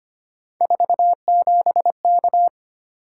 Morse code, heard as a keyed tone, about 25 wpm, 700 Hz. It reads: 47K
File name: Three Letter-Number Combinations - 25wpm